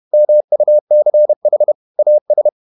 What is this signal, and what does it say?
Morse code, heard as a keyed tone, about 31 wpm, 600 Hz. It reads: MUCH AS
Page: Never 31wpm | CW med Gnister